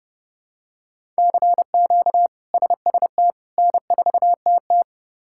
Morse code, heard as a keyed tone, about 30 wpm, 700 Hz. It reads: CQ SST N4TT
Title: K1USN SST QSO Practice - 30wpm